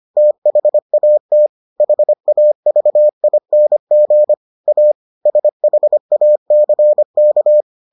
Morse code, heard as a keyed tone, about 25 wpm, 600 Hz. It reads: THAT HAVING A SHACK